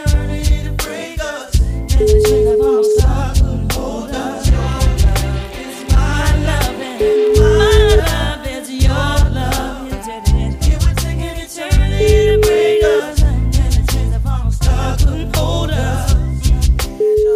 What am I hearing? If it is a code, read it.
Ringback tone (Europe, 425 Hz), with noise.